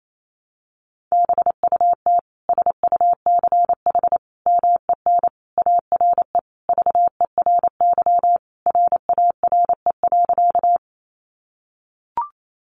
Morse code, heard as a keyed tone, about 28 wpm, 700 Hz. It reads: BUT SUCH MEN ARE VERY RARE.
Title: Sentences from Top 1100 Words 28wpm